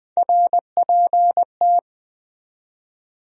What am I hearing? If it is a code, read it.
Morse code, heard as a keyed tone, about 20 wpm, 700 Hz. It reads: RPT